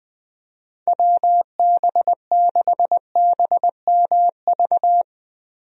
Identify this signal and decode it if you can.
Morse code, heard as a keyed tone, about 21 wpm, 700 Hz. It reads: WB6BMV
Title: USA Call Signs - 6 Characters - Repeated 2x - 20wpm